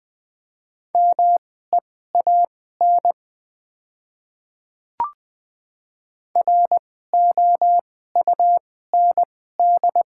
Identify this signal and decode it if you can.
Morse code, heard as a keyed tone, about 20 wpm, spaced as for 15 wpm, 700 Hz. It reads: MEAN ROUND
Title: 2x Character Spacing - Top 200 Words - 20wpm